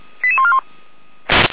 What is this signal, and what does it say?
Five-tone selective call (ZVEI), heard as quick runs of five tones, tones of about 70 ms. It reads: 88141